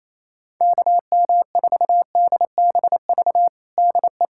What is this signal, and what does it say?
Morse code, heard as a keyed tone, about 27 wpm, 700 Hz. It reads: KM4DBV DE
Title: USA Callsign DE Callsign - 6 Characters - 28wpm